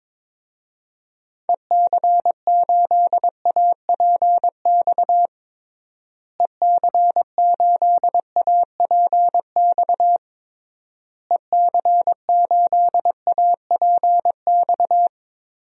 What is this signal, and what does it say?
Morse code, heard as a keyed tone, about 22 wpm, 700 Hz. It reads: EC8APX EC8APX EC8APX
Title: DXCC-100 Call Signs - 6 Characters - Repeated 3x - 22wpm